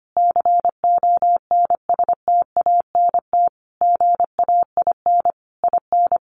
Morse code, heard as a keyed tone, about 25 wpm, 700 Hz. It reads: CONSTANT GAIN IN
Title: Sentences from Top 1500 Words 25wpm